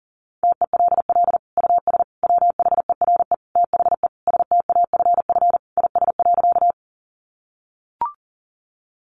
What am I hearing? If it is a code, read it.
Morse code, heard as a keyed tone, about 40 wpm, 700 Hz. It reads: TELL US WHERE THE STAFF IS.
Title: Sentences from Top 1200 Words 40wpm